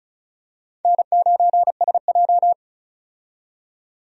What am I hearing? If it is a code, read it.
Morse code, heard as a keyed tone, about 34 wpm, 700 Hz. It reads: N9SJ
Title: USA Call Signs - 4 Characters - 35wpm